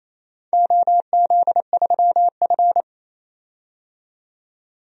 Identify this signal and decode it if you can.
Morse code, heard as a keyed tone, about 28 wpm, 700 Hz. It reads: OZ3F
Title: DXCC-100 Call Signs - 4 Characters - 28wpm